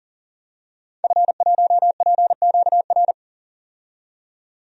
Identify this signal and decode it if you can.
Morse code, heard as a keyed tone, about 39 wpm, 700 Hz. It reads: F1PQR
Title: DXCC-100 Call Signs - 5 Characters - 40wpm